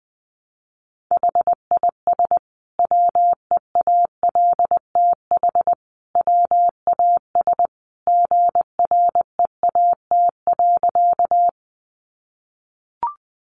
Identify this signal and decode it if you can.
Morse code, heard as a keyed tone, about 19 wpm, 700 Hz. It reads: HIS WEALTH WAS GREAT.
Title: Sentences from Top 1200 Words 20wpm